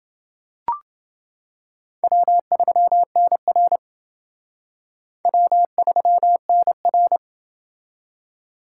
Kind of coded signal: Morse code

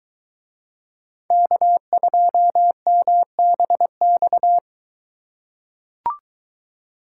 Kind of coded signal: Morse code

